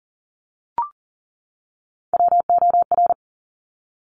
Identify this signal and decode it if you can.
Morse code, heard as a keyed tone, about 38 wpm, 700 Hz. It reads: WOR
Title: Top Three Letter Combinations in Words 40wpm